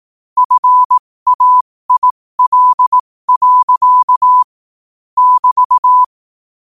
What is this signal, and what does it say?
Morse code, heard as a keyed tone, about 18 wpm, spaced as for 15 wpm, 1000 Hz. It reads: FAIL. <BT>
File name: qotdfiles / 15wpm